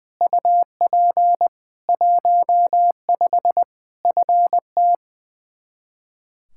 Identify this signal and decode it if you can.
Morse code, heard as a keyed tone, about 21 wpm, 700 Hz. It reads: UP 15 FT
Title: Practice QSO - Exchange 2 - Info - 20wpm